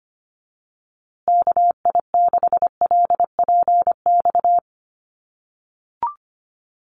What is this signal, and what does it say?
Morse code, heard as a keyed tone, about 25 wpm, 700 Hz. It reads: KI6LPX
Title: USA Call Signs Repeated 3x 25wpm